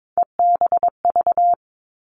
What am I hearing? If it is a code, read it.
Morse code, heard as a keyed tone, about 22 wpm, 700 Hz. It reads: EBV
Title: Three Challenging Letter-Number Combinations 22wpm